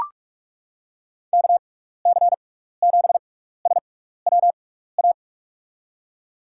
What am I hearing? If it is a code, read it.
Morse code, heard as a keyed tone, about 44 wpm, spaced as for 15 wpm, 700 Hz. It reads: KC7SWA
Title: USA Call Signs - 6x Character Spacing - 45wpm